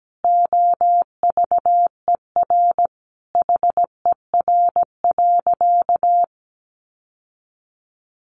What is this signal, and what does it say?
Morse code, heard as a keyed tone, about 17 wpm, 700 Hz. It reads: OVER HER.
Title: Sentences from Top 1400 Words 17wpm